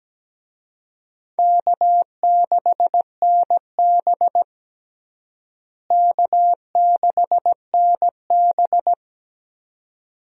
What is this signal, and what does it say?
Morse code, heard as a keyed tone, about 17 wpm, 700 Hz. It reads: K6NB K6NB
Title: USA Call Signs - 4 Characters - Repeated 2x - 17wpm